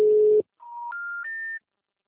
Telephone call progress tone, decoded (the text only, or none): special information tone